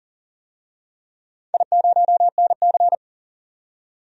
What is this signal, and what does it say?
Morse code, heard as a keyed tone, about 40 wpm, 700 Hz. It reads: I0NC
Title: DXCC-100 Call Signs - 4 Characters - 40wpm